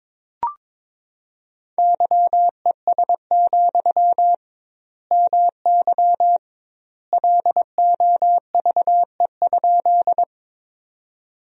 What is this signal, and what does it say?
Morse code, heard as a keyed tone, about 22 wpm, 700 Hz. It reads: YES, MY LOVE?
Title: 2x Word-Spacing - Sentences from Top 500 Words - 22wpm